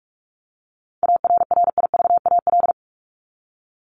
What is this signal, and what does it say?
Morse code, heard as a keyed tone, about 44 wpm, 700 Hz. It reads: ARRIVAL
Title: Top 1751-2000 Words 45wpm